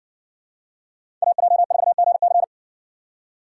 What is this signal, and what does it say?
Morse code, heard as a keyed tone, about 56 wpm, 700 Hz. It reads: AJ4GC